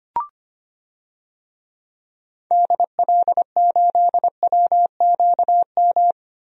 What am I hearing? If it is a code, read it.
Morse code, heard as a keyed tone, about 25 wpm, 700 Hz. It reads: DL8WQM
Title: DXCC-100 Call Signs - 6 Characters - Repeated 2x - 25wpm